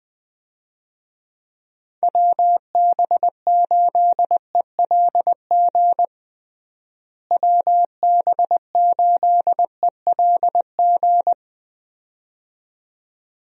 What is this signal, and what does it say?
Morse code, heard as a keyed tone, about 20 wpm, 700 Hz. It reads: WB8ELG WB8ELG
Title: USA Call Signs - 6 Characters - Repeated 2x - 20wpm